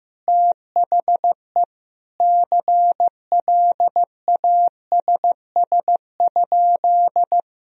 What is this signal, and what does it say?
Morse code, heard as a keyed tone, about 15 wpm, 700 Hz. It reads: THE CLASS?
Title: Sentences from Top 1000 Words - Review Part 2 - 15wpm